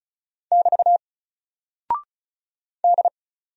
Morse code, heard as a keyed tone, about 34 wpm, 700 Hz. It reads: <BT> D
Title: Single Character (Letter:Number:Punctuation:Prosign) 35wpm